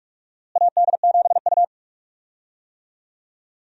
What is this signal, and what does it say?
Morse code, heard as a keyed tone, about 46 wpm, 700 Hz. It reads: AD7U
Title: USA Call Signs - 4 Characters - Repeated 3x - 45wpm